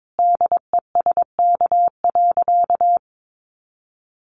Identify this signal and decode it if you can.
Morse code, heard as a keyed tone, about 22 wpm, 700 Hz. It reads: DESK.
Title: A Sound Of Thunder 22wpm